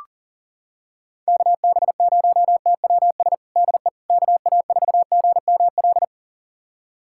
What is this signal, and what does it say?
Morse code, heard as a keyed tone, about 40 wpm, 700 Hz. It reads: KB0TWS DE KA4GML
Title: USA Callsign DE Callsign - 6 Characters - 40wpm